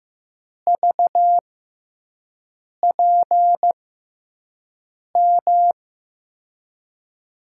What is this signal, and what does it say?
Morse code, heard as a keyed tone, about 16 wpm, spaced as for 5 wpm, 700 Hz. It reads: VPM
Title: USA Call Signs - 6 Characters - 6x Character Spacing - 15wpm